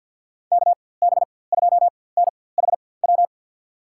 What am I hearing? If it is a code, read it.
Morse code, heard as a keyed tone, about 52 wpm, spaced as for 23 wpm, 700 Hz. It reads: KB2NHW
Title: USA Call Signs - 6 Characters - 4x Character Spacing - 50wpm